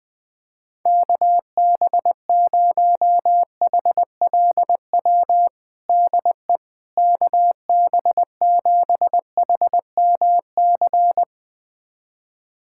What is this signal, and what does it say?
Morse code, heard as a keyed tone, about 21 wpm, 700 Hz. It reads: KB0HLW DE KB7HMC
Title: USA Callsign DE Callsign - 6 Characters - 20wpm